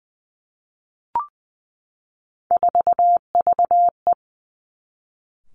Morse code, heard as a keyed tone, about 20 wpm, 700 Hz. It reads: 4VE
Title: Three Challenging Letter-Number Combinations 20wpm